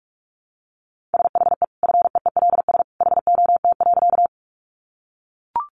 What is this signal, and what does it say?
Morse code, heard as a keyed tone, about 44 wpm, 700 Hz. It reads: SHE FEELS HOT.
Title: Additional Easy Phrases 45wpm